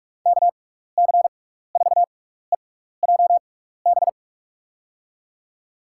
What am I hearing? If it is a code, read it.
Morse code, heard as a keyed tone, about 44 wpm, spaced as for 15 wpm, 700 Hz. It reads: KC4EJB
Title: USA Call Signs - 6x Character Spacing - 45wpm